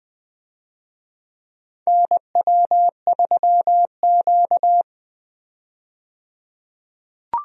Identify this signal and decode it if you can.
Morse code, heard as a keyed tone, about 21 wpm, 700 Hz. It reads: NW3Q
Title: USA Call Signs - 4 Characters - 20wpm